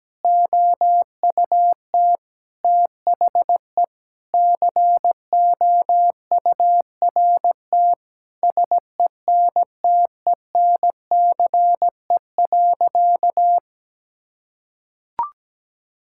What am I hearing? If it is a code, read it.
Morse code, heard as a keyed tone, about 17 wpm, 700 Hz. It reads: OUT THE COURT SENTENCE.
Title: Sentences from Top 1600 Words 17wpm